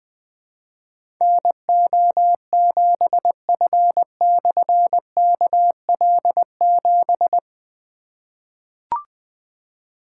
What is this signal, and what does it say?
Morse code, heard as a keyed tone, about 20 wpm, 700 Hz. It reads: NO7F/KL7
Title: Difficult Call Signs 20wpm